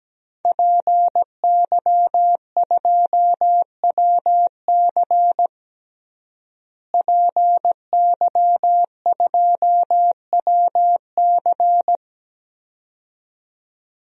Morse code, heard as a keyed tone, about 17 wpm, 700 Hz. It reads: PY2WC PY2WC
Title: DXCC-100 Call Signs - 5 Characters - Repeated 2x - 17wpm